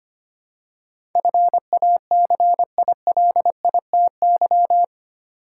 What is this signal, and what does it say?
Morse code, heard as a keyed tone, about 25 wpm, 700 Hz. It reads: FACILITY
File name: Eight Letter Words - 25wpm